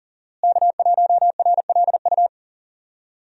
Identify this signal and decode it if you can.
Morse code, heard as a keyed tone, about 40 wpm, 700 Hz. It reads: K1RLU
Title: USA Call Signs - 5 Characters - Repeated 2x - 40wpm